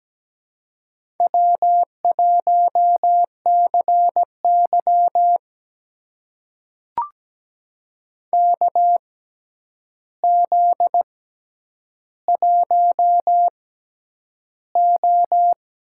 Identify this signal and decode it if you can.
Morse code, heard as a keyed tone, about 17 wpm, spaced as for 9 wpm, 700 Hz. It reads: W1CY KZ1O
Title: USA Call Signs - 4 Characters - 6x Character Spacing - 17wpm